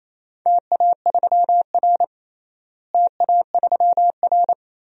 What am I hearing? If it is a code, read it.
Morse code, heard as a keyed tone, about 28 wpm, 700 Hz. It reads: TA3R TA3R
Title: DXCC-100 Call Signs - 4 Characters - Repeated 2x - 28wpm